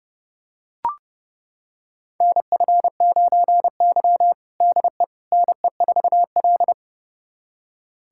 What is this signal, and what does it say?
Morse code, heard as a keyed tone, about 32 wpm, 700 Hz. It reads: NF9Y DE NE4L
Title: USA Callsign DE Callsign - 4 Characters - 30wpm